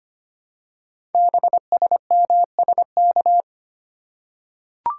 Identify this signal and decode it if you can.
Morse code, heard as a keyed tone, about 25 wpm, 700 Hz. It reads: BSMSK
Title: Code Groups - 5 Letters - 25wpm